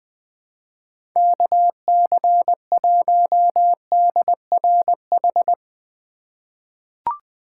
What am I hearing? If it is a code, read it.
Morse code, heard as a keyed tone, about 20 wpm, 700 Hz. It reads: KC1DRH
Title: USA Call Signs - 6 Characters - 2x Character Spacing - 20wpm